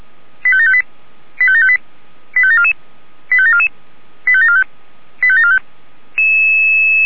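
Five-tone selective call (ZVEI), heard as quick runs of five tones, tones of about 80 ms. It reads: 85758 85758 85744 85744 85745 85745
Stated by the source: ZVEI five-tone selective call 85725 received by SDR.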